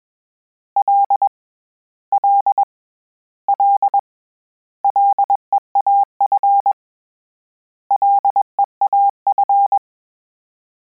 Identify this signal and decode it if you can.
Morse code, heard as a keyed tone, about 21 wpm, 800 Hz. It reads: L L L LEAF LEAF
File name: Combinations / Combinations-mp3s / Combinations@20